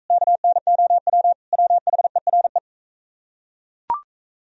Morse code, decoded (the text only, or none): KNOW WHERE